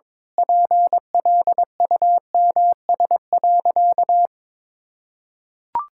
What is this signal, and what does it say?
Morse code, heard as a keyed tone, about 21 wpm, 700 Hz. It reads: PLUMS.
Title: Pangram Challenge v2 - 22wpm